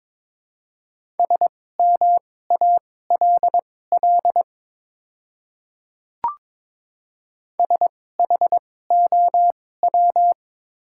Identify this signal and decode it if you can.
Morse code, heard as a keyed tone, about 22 wpm, spaced as for 16 wpm, 700 Hz. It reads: SMALL SHOW
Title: 2x Character Spacing - Top 200 Words - 22wpm